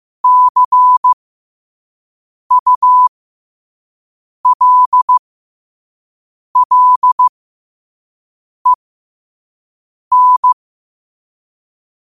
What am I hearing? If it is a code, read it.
Morse code, heard as a keyed tone, about 15 wpm, spaced as for 5 wpm, 1000 Hz. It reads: CULLEN
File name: qotdfiles / 5wpm